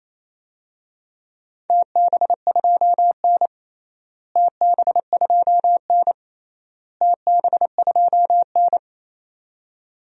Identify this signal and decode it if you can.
Morse code, heard as a keyed tone, about 28 wpm, 700 Hz. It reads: TB2N TB2N TB2N
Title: DXCC-100 Call Signs - 4 Characters - Repeated 3x - 28wpm